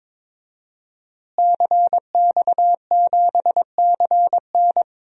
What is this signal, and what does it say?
Morse code, heard as a keyed tone, about 22 wpm, 700 Hz. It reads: CX7CN
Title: DXCC-100 Call Signs - 5 Characters - Repeated 2x - 22wpm